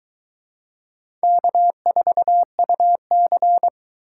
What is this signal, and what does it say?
Morse code, heard as a keyed tone, about 24 wpm, 700 Hz. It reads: K4UC